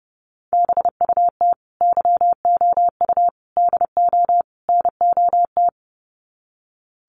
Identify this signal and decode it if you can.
Morse code, heard as a keyed tone, about 30 wpm, 700 Hz. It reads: BUT YOU DO NOT
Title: Sets of 4 Words 30wpm